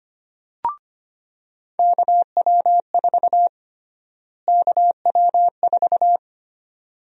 Morse code, heard as a keyed tone, about 25 wpm, 700 Hz. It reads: KW4 KW4
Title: USA Call Sign Prefixes - Repeated 2x - 25wpm